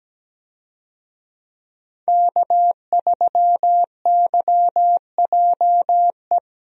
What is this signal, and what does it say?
Morse code, heard as a keyed tone, about 17 wpm, 700 Hz. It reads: K3YJE
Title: USA Call Signs - 5 Characters - 17wpm